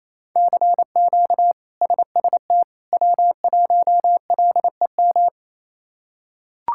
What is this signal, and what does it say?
Morse code, heard as a keyed tone, about 28 wpm, 700 Hz. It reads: CQ SST W1LEM
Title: K1USN SST QSO Practice - 28wpm